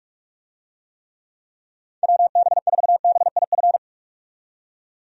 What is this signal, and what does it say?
Morse code, heard as a keyed tone, about 44 wpm, 700 Hz. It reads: WB4BIF
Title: USA Call Signs - 6 Characters - 45wpm